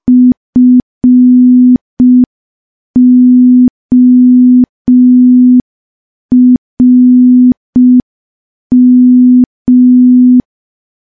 Morse code, heard as a keyed tone, about 5 wpm, 250 Hz. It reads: FORM
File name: morse / morse-training-05WPM